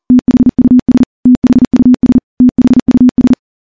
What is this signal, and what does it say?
Morse code, heard as a keyed tone, about 40 wpm, 250 Hz. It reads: THUS THUS THUS
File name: morse / morse-training-40WPM